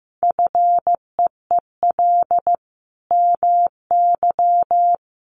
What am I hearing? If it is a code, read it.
Morse code, heard as a keyed tone, about 15 wpm, 700 Hz. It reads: FEEL MY
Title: Sentences from Top 1100 Words 15wpm